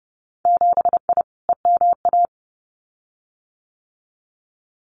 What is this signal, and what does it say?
Morse code, heard as a keyed tone, about 30 wpm, 700 Hz. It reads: 7I EMA
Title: Winter Field Day Exchanges Encore 30wpm